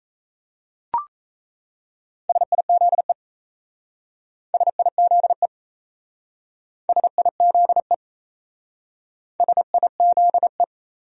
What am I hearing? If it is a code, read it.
Morse code, heard as a keyed tone, about 37 wpm, 700 Hz. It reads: SIZE SIZE SIZE SIZE